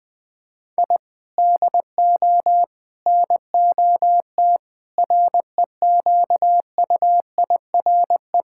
Morse code, heard as a keyed tone, about 21 wpm, 700 Hz. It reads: I DO NOT REQUIRE
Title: Sentences from Top 1000 Words - Review Part 6 - 20wpm